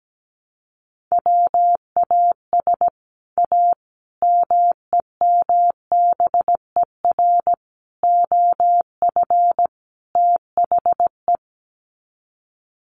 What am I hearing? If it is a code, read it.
Morse code, heard as a keyed tone, about 17 wpm, 700 Hz. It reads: WAS A MEMBER OF THE
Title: Sets of 5 Words 17wpm